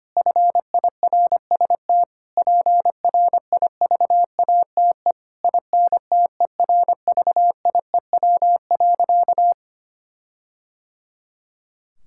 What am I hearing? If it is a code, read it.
Morse code, heard as a keyed tone, about 25 wpm, 700 Hz. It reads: FIRST PRIVATE INTERVIEW.
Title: Sentences from Top 1600 Words 25wpm